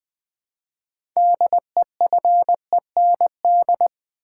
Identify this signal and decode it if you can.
Morse code, heard as a keyed tone, about 20 wpm, 700 Hz. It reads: DEFEND
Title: Build A Word - 2 Letters At A Time - 6-Letter Words - 20wpm